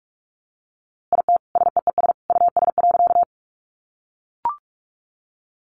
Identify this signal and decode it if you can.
Morse code, heard as a keyed tone, about 44 wpm, 700 Hz. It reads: IT SEES US.
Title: A Sound Of Thunder 45wpm